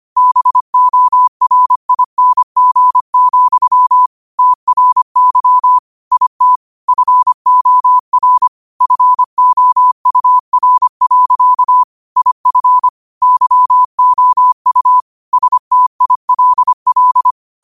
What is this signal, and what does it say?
Morse code, heard as a keyed tone, about 25 wpm, 1000 Hz. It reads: BORING, TRY IT FOR FOUR. IF YOU STILL
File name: qotdfiles / 25wpm